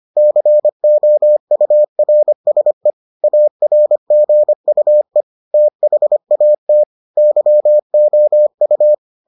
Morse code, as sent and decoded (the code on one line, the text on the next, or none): -.-. --- ..- .-. ... / .- .-. --. ..- . / - .... .- - / -.-- --- ..-
COURSE ARGUE THAT YOU